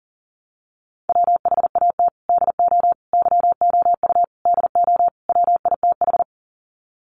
Morse code, heard as a keyed tone, about 40 wpm, 700 Hz. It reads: WHAT DO YOU DO WITH
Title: Sets of 5 Words Encore2 40wpm